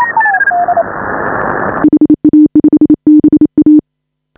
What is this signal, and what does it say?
Morse code, heard as a keyed tone, about 29 wpm, 300 Hz, with radio noise. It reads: HA5BA